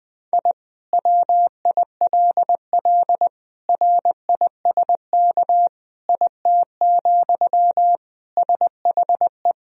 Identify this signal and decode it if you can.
Morse code, heard as a keyed tone, about 20 wpm, 700 Hz. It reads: I WILL RISK IT, SHE
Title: Sentences from Top 1000 Words - Review Part 4 - 20wpm